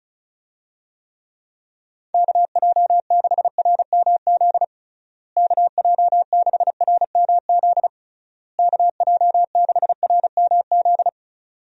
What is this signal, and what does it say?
Morse code, heard as a keyed tone, about 34 wpm, 700 Hz. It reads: KJ6RMZ KJ6RMZ KJ6RMZ
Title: USA Call Signs - 6 Characters - Repeated 3x - 35wpm